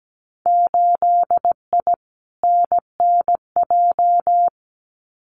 Morse code, heard as a keyed tone, about 17 wpm, 700 Hz. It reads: 8I NNJ
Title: Winter Field Day Exchanges Encore 17wpm